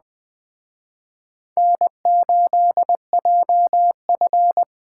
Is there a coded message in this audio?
Yes: Morse code from 1.6 to 4.6 s, heard as a keyed tone, about 20 wpm, 700 Hz. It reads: N8JF